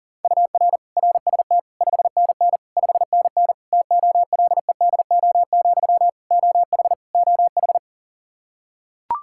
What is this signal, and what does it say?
Morse code, heard as a keyed tone, about 40 wpm, 700 Hz. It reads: UR RST 5NN 5NN TOLEDO, OH OH
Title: Practice QSO - Exchange 1 - Beginning - 40wpm